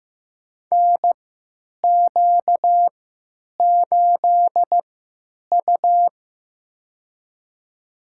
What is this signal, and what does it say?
Morse code, heard as a keyed tone, about 15 wpm, spaced as for 8 wpm, 700 Hz. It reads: NQ8U
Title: USA Call Signs - 4 Characters - 3x Character Spacing - 15wpm